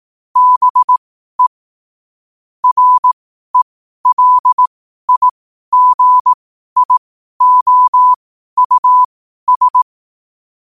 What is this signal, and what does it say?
Morse code, heard as a keyed tone, about 18 wpm, spaced as for 12 wpm, 1000 Hz. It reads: BE RELIGIOUS
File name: qotdfiles / 12wpm